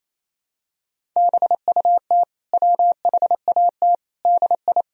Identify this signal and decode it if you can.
Morse code, heard as a keyed tone, about 28 wpm, 700 Hz. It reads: BUT WHAT DI
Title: Sentences from Top 100 Words and Abbreviations 28wpm